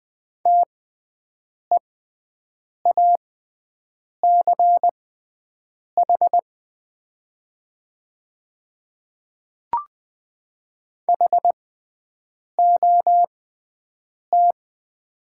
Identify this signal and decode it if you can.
Morse code, heard as a keyed tone, about 20 wpm, spaced as for 7 wpm, 700 Hz. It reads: TEACH HOT
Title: 6x Character Spacing - Top 500 Words - 20wpm